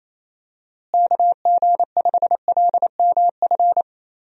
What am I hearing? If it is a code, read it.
Morse code, heard as a keyed tone, about 28 wpm, 700 Hz. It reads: KG5LMF
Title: USA Call Signs - 6 Characters - Repeated 2x - 28wpm